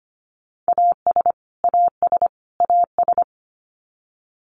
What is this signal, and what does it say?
Morse code, heard as a keyed tone, about 25 wpm, 700 Hz. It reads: AS AS AS
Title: Top 300 Words Repeated 25wpm